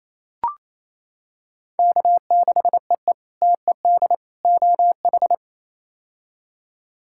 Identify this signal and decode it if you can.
Morse code, heard as a keyed tone, about 28 wpm, 700 Hz. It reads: K6EE TED OH
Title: K1USN SST QSO Practice - 28wpm